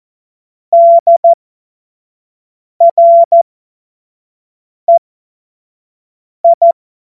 Morse code, heard as a keyed tone, about 14 wpm, spaced as for 5 wpm, 680 Hz. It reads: DREI
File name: Morsekurs / Witze / Witz 10 Juni 2025 25 F70